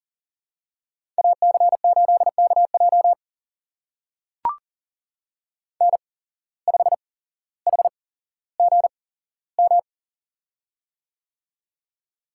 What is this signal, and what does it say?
Morse code, heard as a keyed tone, about 42 wpm, spaced as for 17 wpm, 700 Hz. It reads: AC8KJ N5HGM
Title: USA Call Signs - 5 Characters - 8x Character Spacing - 40wpm